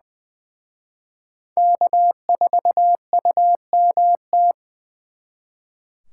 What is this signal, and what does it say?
Morse code, heard as a keyed tone, about 24 wpm, 700 Hz. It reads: K4UMT
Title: Speed Racing - USA 1x3 Call Signs 20wpm